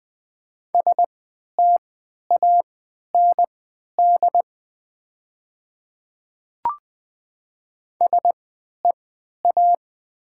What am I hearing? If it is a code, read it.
Morse code, heard as a keyed tone, about 20 wpm, spaced as for 11 wpm, 690 Hz. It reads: STAND SEA
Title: 3x Character Spacing - Top 300 Words - 20wpm